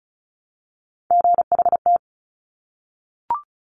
Morse code, heard as a keyed tone, about 36 wpm, 700 Hz. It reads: GHT
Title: Top Three Letter Combinations in Words 35wpm